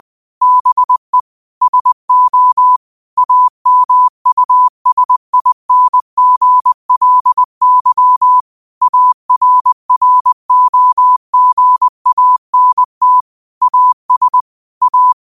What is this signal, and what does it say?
Morse code, heard as a keyed tone, about 20 wpm, 1000 Hz. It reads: BE SO AMUSINGLY ARROGANT AS A
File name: qotdfiles / 20wpm